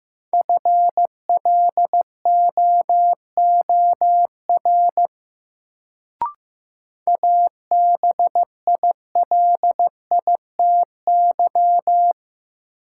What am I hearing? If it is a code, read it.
Morse code, heard as a keyed tone, about 15 wpm, 700 Hz. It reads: FLOOR ABILITY
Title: Sets of 3 Words - New Word Drill - 15wpm